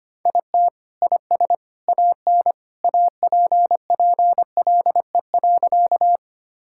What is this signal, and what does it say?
Morse code, heard as a keyed tone, about 25 wpm, 700 Hz. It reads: IT IS AN APPLE.
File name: Sentences with 2-Word Phrases - 25wpm